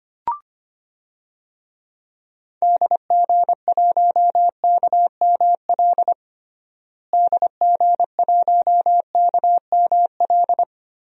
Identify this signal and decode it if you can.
Morse code, heard as a keyed tone, about 25 wpm, 700 Hz. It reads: DG1KML DG1KML
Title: DXCC-100 Call Signs - 6 Characters - Repeated 3x - 25wpm